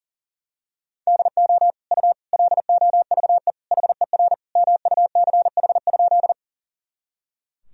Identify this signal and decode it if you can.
Morse code, heard as a keyed tone, about 40 wpm, 700 Hz. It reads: DO U LOVE HER MUCH?
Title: Sentences from Top 400 Words and Abbreviations 40wpm